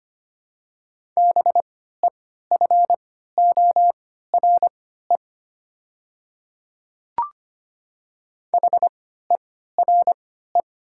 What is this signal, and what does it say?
Morse code, heard as a keyed tone, about 25 wpm, spaced as for 14 wpm, 700 Hz. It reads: BEFORE HERE
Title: 3x Character Spacing - Top 200 Words - 25wpm